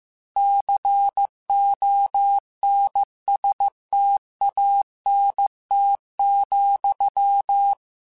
Morse code, heard as a keyed tone, about 15 wpm, 800 Hz. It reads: CONSTANT,